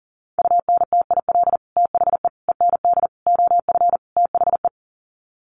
Morse code, heard as a keyed tone, about 40 wpm, 700 Hz. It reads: UNTIL THE END OF THE
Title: Sets of 5 Words Encore 40wpm